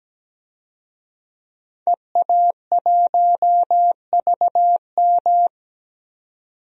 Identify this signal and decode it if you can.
Morse code, heard as a keyed tone, about 17 wpm, 700 Hz. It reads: EA1VM
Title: DXCC-100 Call Signs - 5 Characters - 17wpm